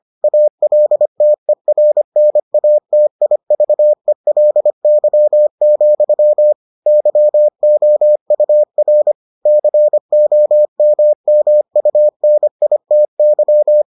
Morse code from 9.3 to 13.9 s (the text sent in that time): COMMUNITY